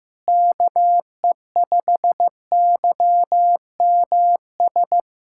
Morse code, heard as a keyed tone, about 15 wpm, 700 Hz. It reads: KE5YMS
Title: USA Callsign DE Callsign - 6 Characters - 15wpm